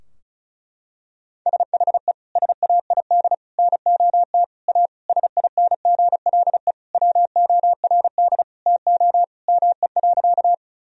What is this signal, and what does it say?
Morse code, heard as a keyed tone, about 36 wpm, 700 Hz. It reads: SHE SAID NOT A SINGLE WORD TO ME.
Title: Sentences from Top 1000 Words - Review Part 4 - 35wpm